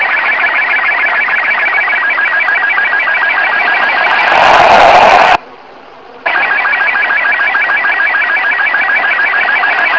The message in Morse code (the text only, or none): none